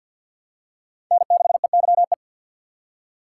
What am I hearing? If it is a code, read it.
Morse code, heard as a keyed tone, about 51 wpm, 700 Hz. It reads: N6EYE
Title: USA Call Signs - 5 Characters - 4x Character Spacing - 50wpm